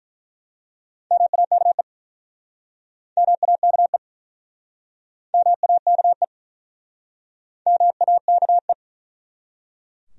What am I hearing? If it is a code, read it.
Morse code, heard as a keyed tone, about 48 wpm, 700 Hz. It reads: MAKE MAKE MAKE MAKE